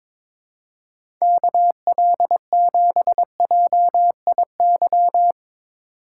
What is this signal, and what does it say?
Morse code, heard as a keyed tone, about 21 wpm, 700 Hz. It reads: KL7JIY
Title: USA Call Signs - 6 Characters - 4x Character Spacing - 22wpm